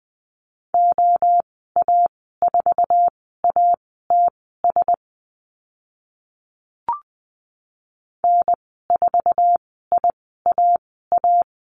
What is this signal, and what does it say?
Morse code, heard as a keyed tone, about 20 wpm, spaced as for 14 wpm, 700 Hz. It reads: OA4ATS N4IAA
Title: 062 Lesson 11g - Letter h - Call Signs 20wpm